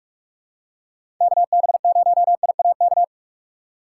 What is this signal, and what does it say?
Morse code, heard as a keyed tone, about 46 wpm, 700 Hz. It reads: KB0IAK
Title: USA Call Signs - 6 Characters - 6x Character Spacing - 45wpm